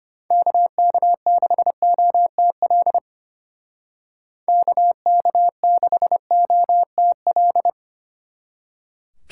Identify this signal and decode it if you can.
Morse code, heard as a keyed tone, about 34 wpm, 700 Hz. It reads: KK6OTL KK6OTL